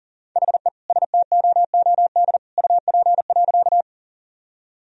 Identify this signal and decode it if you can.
Morse code, heard as a keyed tone, about 37 wpm, 700 Hz. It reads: HE STOOD UP.
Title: Sentences from Top 1000 Words - Review Part 2 - 40wpm